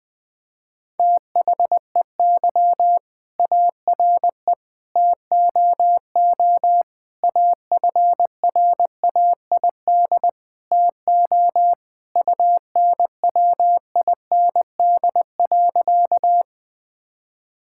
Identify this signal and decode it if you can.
Morse code, heard as a keyed tone, about 20 wpm, 700 Hz. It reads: THEY ARE TOO AFRAID TO UNWIND.
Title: All-Age Appropriate Jokes - 20wpm